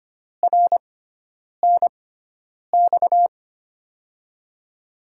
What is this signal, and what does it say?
Morse code, heard as a keyed tone, about 25 wpm, spaced as for 9 wpm, 700 Hz. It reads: RNX
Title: USA Call Signs - 6x Character Spacing - 25wpm